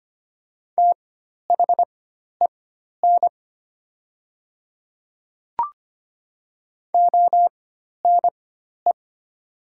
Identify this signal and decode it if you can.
Morse code, heard as a keyed tone, about 25 wpm, spaced as for 12 wpm, 700 Hz. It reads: THEN ONE
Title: 4x Character Spacing - Top 100 Words - 25wpm